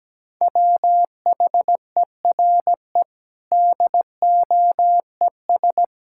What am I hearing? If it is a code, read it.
Morse code, heard as a keyed tone, about 17 wpm, 700 Hz. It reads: WHERE DOES
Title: Sentences from Top 1000 Words - Review Part 6 - 17wpm